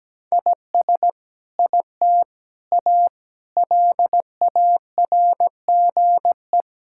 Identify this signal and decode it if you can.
Morse code, heard as a keyed tone, about 17 wpm, 700 Hz. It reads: IS IT A LARGE
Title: Sentences from Top 1000 Words - Review Part 6 - 17wpm